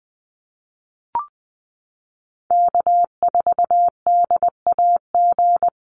Morse code, heard as a keyed tone, about 20 wpm, 700 Hz. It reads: K4DAG
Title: USA Call Signs Repeated 3x 20wpm